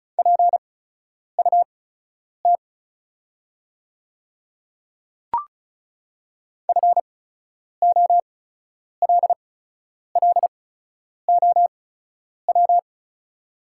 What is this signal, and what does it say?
Morse code, heard as a keyed tone, about 34 wpm, spaced as for 10 wpm, 690 Hz. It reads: PUT FOLLOW
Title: 8x Character Spacing - Top 300 Words - 35wpm